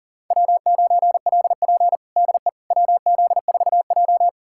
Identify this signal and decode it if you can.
Morse code, heard as a keyed tone, about 40 wpm, 690 Hz. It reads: W9LP DE WZ4J
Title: USA Callsign DE Callsign - 4 Characters - 40wpm